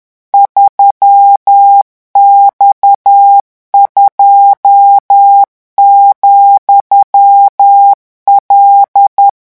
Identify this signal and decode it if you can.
Morse code, heard as a keyed tone, about 10 wpm, 800 Hz. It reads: 3X2,L
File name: morse mp3 / revision13-vitesse 10 mots